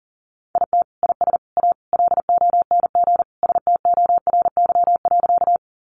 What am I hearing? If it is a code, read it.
Morse code, heard as a keyed tone, about 40 wpm, 700 Hz. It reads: IT IS A LONG STORY.
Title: Sentences from Top 300 Words 40wpm